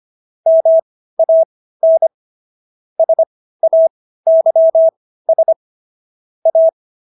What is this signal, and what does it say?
Morse code, heard as a keyed tone, about 25 wpm, spaced as for 15 wpm, 650 Hz. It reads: MAN SAYS A
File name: SS110 15